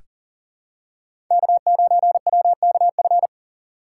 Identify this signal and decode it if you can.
Morse code, heard as a keyed tone, about 38 wpm, 700 Hz. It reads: K9WKF
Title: USA Call Signs - 5 Characters - 3x Character Spacing - 40wpm